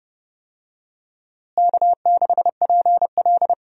Morse code, heard as a keyed tone, about 31 wpm, 700 Hz. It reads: K6PL
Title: USA Call Signs - 4 Characters - Repeated 2x - 30wpm